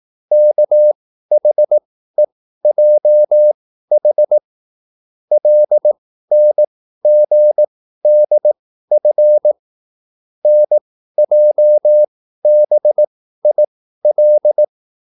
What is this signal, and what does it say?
Morse code, heard as a keyed tone, about 18 wpm, spaced as for 13 wpm, 600 Hz. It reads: KHEJH LNGDF NJBIL
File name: QSL / CW lesson